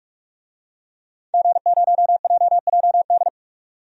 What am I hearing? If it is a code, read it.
Morse code, heard as a keyed tone, about 45 wpm, 700 Hz. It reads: G0JJD
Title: DXCC-100 Call Signs - 5 Characters - 45wpm